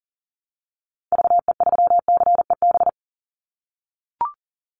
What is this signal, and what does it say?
Morse code, heard as a keyed tone, about 40 wpm, 700 Hz. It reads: VE3CEB
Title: Call Signs - 40wpm